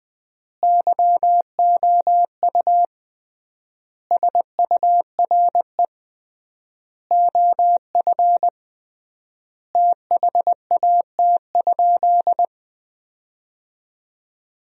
Morse code, heard as a keyed tone, about 21 wpm, 700 Hz. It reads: YOU SURE OF THAT?
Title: 3x Word-Spacing - Sentences from Top 300 Words - 20wpm